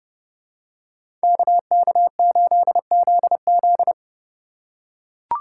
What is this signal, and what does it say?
Morse code, heard as a keyed tone, about 29 wpm, 700 Hz. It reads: KK8ZZ